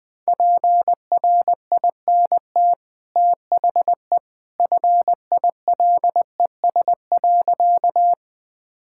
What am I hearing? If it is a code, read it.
Morse code, heard as a keyed tone, about 20 wpm, 700 Hz. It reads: PRINT THE FILES.
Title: Easy Sentences with Hints - 20wpm